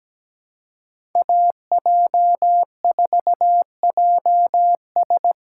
Morse code, heard as a keyed tone, about 17 wpm, 700 Hz. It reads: AJ4JS
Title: USA Call Signs - 5 Characters - 3x Character Spacing - 17wpm